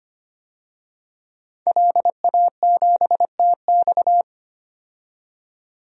Morse code, heard as a keyed tone, about 25 wpm, 700 Hz. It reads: LA7TX
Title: DXCC-100 Call Signs - 5 Characters - 25wpm